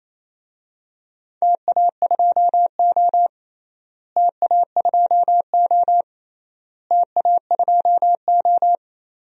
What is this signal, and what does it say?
Morse code, heard as a keyed tone, about 28 wpm, 700 Hz. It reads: TA2O TA2O TA2O
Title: DXCC-100 Call Signs - 4 Characters - Repeated 3x - 28wpm